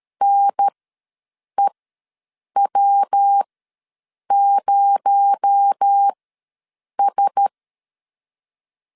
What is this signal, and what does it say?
Morse code, heard as a keyed tone, about 13 wpm, spaced as for 7 wpm, 800 Hz. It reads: NEW0S
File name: LEZIONI / 6 PKX0 / 6 PKX0 ws 12wpm 3448